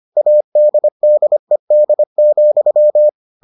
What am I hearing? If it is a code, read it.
Morse code, heard as a keyed tone, about 25 wpm, 600 Hz. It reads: ADDED,